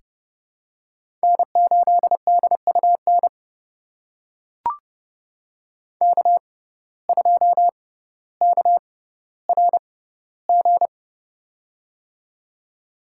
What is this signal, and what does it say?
Morse code, heard as a keyed tone, about 32 wpm, spaced as for 16 wpm, 700 Hz. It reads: N8DUN K2KRG
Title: USA Call Signs - 5 Characters - 6x Character Spacing - 30wpm